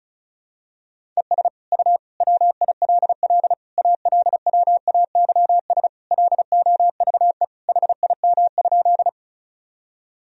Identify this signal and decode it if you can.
Morse code, heard as a keyed tone, about 36 wpm, 700 Hz. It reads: ES U WILL ALWAYS LOVE HIM?
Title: Sentences from Top 400 Words and Abbreviations 35wpm